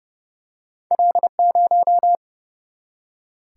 Morse code, heard as a keyed tone, about 30 wpm, 700 Hz. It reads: L0
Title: Single-Letter Single-Number 30wpm